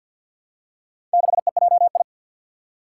Morse code, heard as a keyed tone, about 51 wpm, 700 Hz. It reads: 6EJI
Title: Four Challenging Letter-Number Combinations 50wpm